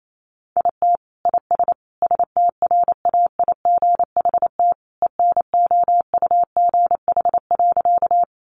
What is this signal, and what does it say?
Morse code, heard as a keyed tone, about 28 wpm, 700 Hz. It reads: IT IS STRAIGHT ENOUGH.
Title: Sentences from Top 1000 Words 28wpm